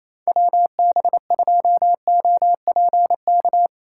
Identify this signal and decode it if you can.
Morse code, heard as a keyed tone, about 34 wpm, 700 Hz. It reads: WB2OPK
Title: Speed Racing - USA Call Signs 28wpm